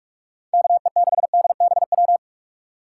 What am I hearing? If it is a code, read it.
Morse code, heard as a keyed tone, about 44 wpm, 700 Hz. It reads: KE6DBW